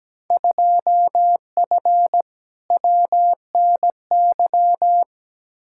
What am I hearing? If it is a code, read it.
Morse code, heard as a keyed tone, about 17 wpm, 700 Hz. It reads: 2F WNY
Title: Field Day Exchanges Encore - 17wpm